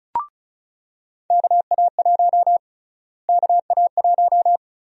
Morse code, heard as a keyed tone, about 36 wpm, 700 Hz. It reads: KA1 KA1
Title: USA Call Sign Prefixes - Repeated 2x - 35wpm